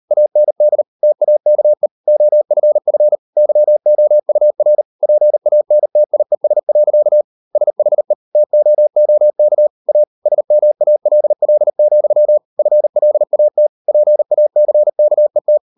Morse code, heard as a keyed tone, about 38 wpm, 600 Hz. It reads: AND TAKE OFF YOUR PANTIES. SHE TOOK A SMALL, FLAT PACKET